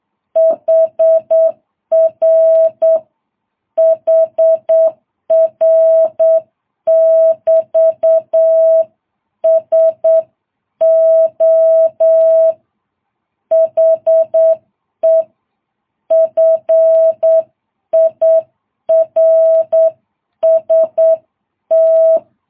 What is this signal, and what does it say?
Morse code, heard as a keyed tone, about 8 wpm, 650 Hz. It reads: HR HR <BT> SO HE FIRST
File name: jh8cbh